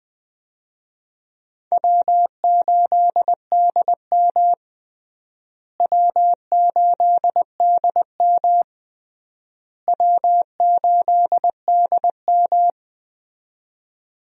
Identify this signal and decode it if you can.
Morse code, heard as a keyed tone, about 21 wpm, 700 Hz. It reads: W8DM W8DM W8DM
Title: USA Call Signs - 4 Characters - Repeated 3x - 20wpm